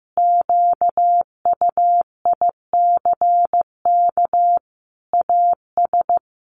Morse code, heard as a keyed tone, about 15 wpm, 700 Hz. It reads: QUICK AS
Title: Sentences from Top 500 Words 15wpm